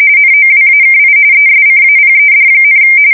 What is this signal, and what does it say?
RTTY, about 45.45 baud. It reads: NOWAKI ESCAPED,
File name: fsk170 / USS-New-Jersey